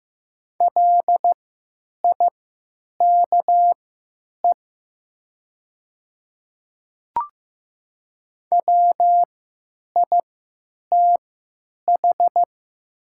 Morse code, heard as a keyed tone, about 15 wpm, spaced as for 9 wpm, 700 Hz. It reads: LIKE WITH
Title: 3x Character Spacing - Top 200 Words - 15wpm